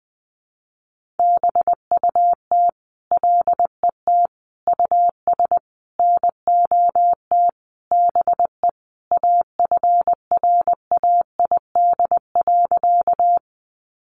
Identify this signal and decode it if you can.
Morse code, heard as a keyed tone, about 20 wpm, 700 Hz. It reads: BUT LET US NOT BE AFRAID.
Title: Sentences from Top 1500 Words 20wpm